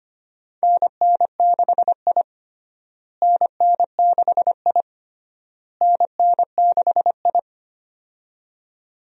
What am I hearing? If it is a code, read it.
Morse code, heard as a keyed tone, about 25 wpm, 700 Hz. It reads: NN6I NN6I NN6I
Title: USA Call Signs - 4 Characters - Repeated 3x - 25wpm